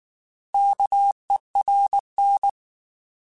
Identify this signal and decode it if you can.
Morse code, heard as a keyed tone, about 19 wpm, 800 Hz. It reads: KERN